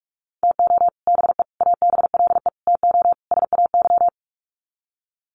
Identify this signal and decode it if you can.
Morse code, heard as a keyed tone, about 44 wpm, 700 Hz. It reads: TO BE ABLE TO SAY
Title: Sets of 5 Words Encore2 45wpm